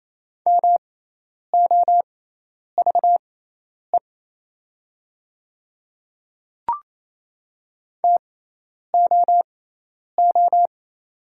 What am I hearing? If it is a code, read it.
Morse code, heard as a keyed tone, about 27 wpm, spaced as for 10 wpm, 700 Hz. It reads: MOVE TOO
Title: 6x Character Spacing - Top 200 Words - 28wpm